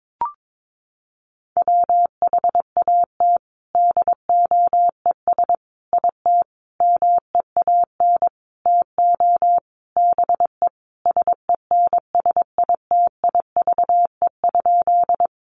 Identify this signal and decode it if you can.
Morse code, heard as a keyed tone, about 22 wpm, 700 Hz. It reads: WHAT DOES IT MEAN TO BE SENSITIVE?
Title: Sentences from Top 1400 Words 22wpm